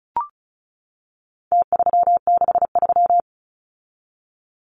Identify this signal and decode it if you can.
Morse code, heard as a keyed tone, about 34 wpm, 700 Hz. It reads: T363
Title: Numbers - Contest Style - 4-Digit 35wpm